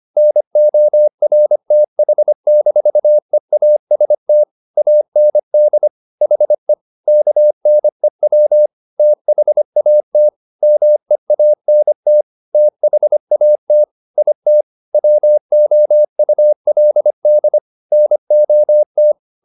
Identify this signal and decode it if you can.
Morse code, heard as a keyed tone, about 25 wpm, 600 Hz. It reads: NORTH-EAST AND HE KNEW THAT MEANT THAT IT WOULD NOT